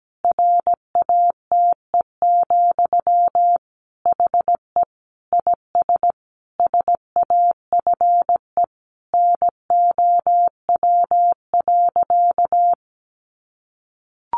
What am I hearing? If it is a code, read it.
Morse code, heard as a keyed tone, about 17 wpm, 700 Hz. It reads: RATE, HE IS SAFE NOW.
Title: Sentences from Top 900 Words 17wpm